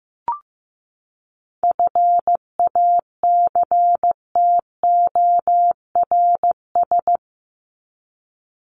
Morse code, heard as a keyed tone, about 15 wpm, 700 Hz. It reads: FACTORS
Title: ES:S Suffix - Top 100 Words with Base Words 15wpm